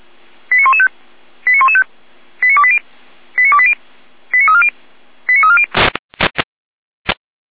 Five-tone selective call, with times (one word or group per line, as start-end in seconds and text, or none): 0.5-0.9 89115
1.5-1.8 89115
2.4-2.8 89280
3.4-3.7 89280
4.3-4.7 89340
5.3-5.7 89340